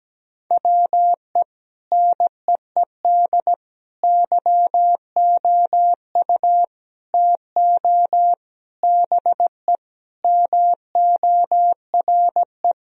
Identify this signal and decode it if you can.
Morse code, heard as a keyed tone, about 17 wpm, 700 Hz. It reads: WE NEED YOU TO BE MORE
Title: Sentences from Top 1000 Words - Review Part 3 - 17wpm